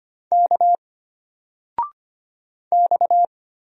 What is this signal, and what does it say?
Morse code, heard as a keyed tone, about 25 wpm, 700 Hz. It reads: K X
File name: Single-Letter-Number-25wpm